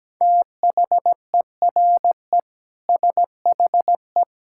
Morse code, heard as a keyed tone, about 17 wpm, 700 Hz. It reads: THERE SHE
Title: Sentences from Top 1600 Words 17wpm